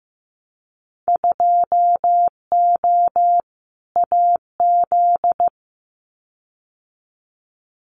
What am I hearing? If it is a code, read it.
Morse code, heard as a keyed tone, about 15 wpm, 700 Hz. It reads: 2O AZ
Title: Winter Field Day Exchanges 15wpm